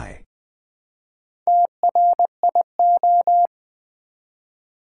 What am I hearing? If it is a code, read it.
Morse code, heard as a keyed tone, about 20 wpm, 700 Hz. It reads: TRIO
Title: Build A Word - 1 Letter At A Time - 4-Letter Words - 20wpm